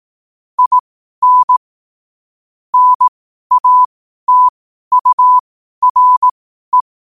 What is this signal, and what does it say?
Morse code, heard as a keyed tone, about 18 wpm, spaced as for 12 wpm, 1000 Hz. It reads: IN NATURE
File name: qotdfiles / 12wpm